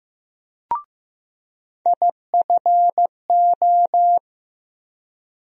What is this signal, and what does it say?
Morse code, heard as a keyed tone, about 15 wpm, 700 Hz. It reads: IFO
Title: USA Call Sign Postfixes - 3 Letters 15wpm